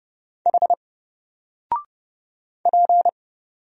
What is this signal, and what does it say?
Morse code, heard as a keyed tone, about 30 wpm, 700 Hz. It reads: H P
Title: Single Letter-Number 30wpm